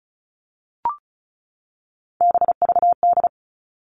Morse code, heard as a keyed tone, about 34 wpm, 700 Hz. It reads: BVD
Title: All Cut Numbers - 3-Digit 35wpm